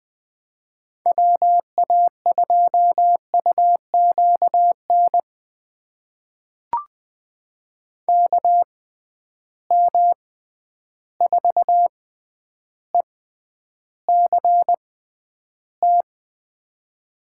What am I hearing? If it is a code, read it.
Morse code, heard as a keyed tone, about 21 wpm, spaced as for 10 wpm, 700 Hz. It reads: WA2UQN KM4ECT
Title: USA Call Signs - 6 Characters - 6x Character Spacing - 20wpm